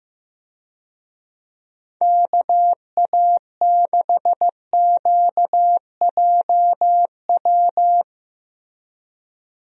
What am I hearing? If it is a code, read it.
Morse code, heard as a keyed tone, about 15 wpm, 700 Hz. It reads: KA6QJW
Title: USA Call Signs - 6 Characters - 15wpm